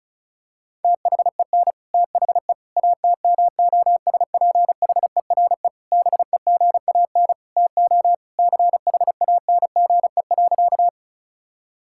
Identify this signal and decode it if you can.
Morse code, heard as a keyed tone, about 34 wpm, 700 Hz. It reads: THEN THE ATMOSPHERE BEGAN TO CHANGE.
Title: Sentences from Top 1600 Words 35wpm